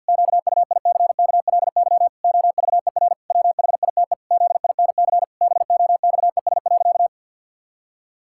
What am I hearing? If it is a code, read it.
Morse code, heard as a keyed tone, about 51 wpm, 700 Hz. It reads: QUICKLY OVER WHITE ZINC BOXES.
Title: Pangram Challenge v2 - 50wpm